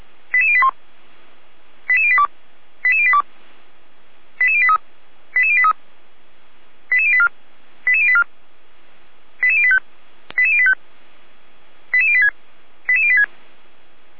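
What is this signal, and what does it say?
Five-tone selective call (ZVEI), heard as quick runs of five tones, tones of about 70 ms. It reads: 88081 88082 88082 88083 88083 88084 88084 88085 88085 88086 88086
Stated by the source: Received on an SDR as a ZVEI five-tone call, address 88080.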